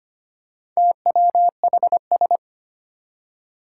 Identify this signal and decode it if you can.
Morse code, heard as a keyed tone, about 25 wpm, 700 Hz. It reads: TWHS
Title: Four Challenging Letter-Number Combinations 25wpm